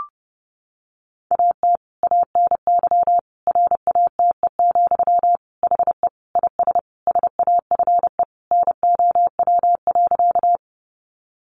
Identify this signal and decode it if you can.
Morse code, heard as a keyed tone, about 30 wpm, 700 Hz. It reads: AT ANY RATE, HE IS SAFE NOW.
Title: Sentences from Top 900 Words 30wpm